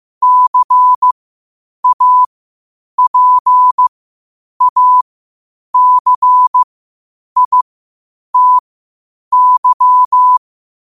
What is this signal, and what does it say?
Morse code, heard as a keyed tone, about 15 wpm, spaced as for 8 wpm, 1000 Hz. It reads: CAPACITY
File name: qotdfiles / 8wpm